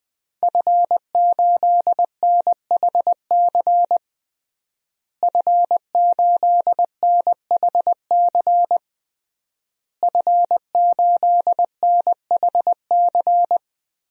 Morse code, heard as a keyed tone, about 20 wpm, 700 Hz. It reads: F8NHC F8NHC F8NHC
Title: DXCC-100 Call Signs - 5 Characters - Repeated 3x - 20wpm